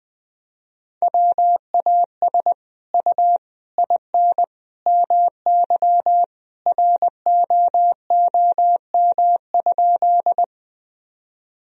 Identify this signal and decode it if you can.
Morse code, heard as a keyed tone, about 20 wpm, 700 Hz. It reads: WAS U IN MY ROOM?
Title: Sentences from Top 300 Words and Abbreviations 20wpm